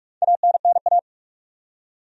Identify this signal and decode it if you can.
Morse code, heard as a keyed tone, about 46 wpm, 700 Hz. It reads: ANNA
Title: USA Common Names - Max 4 Letters - 45wpm